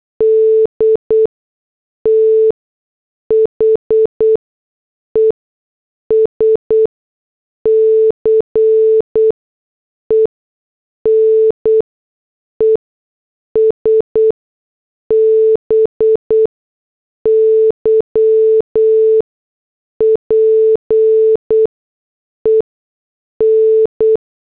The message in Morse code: D THE SCENES BY PEN